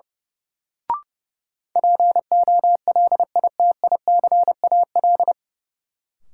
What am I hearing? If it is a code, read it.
Morse code, heard as a keyed tone, about 30 wpm, 700 Hz. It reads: POLITICAL
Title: Sentences from Top 800 Words - New Word Drill - 30wpm